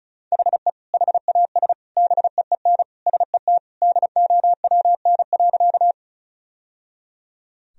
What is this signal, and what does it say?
Morse code, heard as a keyed tone, about 36 wpm, 700 Hz. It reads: HE HAS BEEN SET DOWN.
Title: Sentences from Top 1000 Words - Review Part 1 - 35wpm